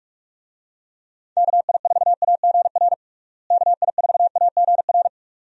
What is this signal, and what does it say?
Morse code, heard as a keyed tone, about 46 wpm, 700 Hz. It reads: KI4AGR KI4AGR
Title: USA Call Signs - 6 Characters - Repeated 2x - 45wpm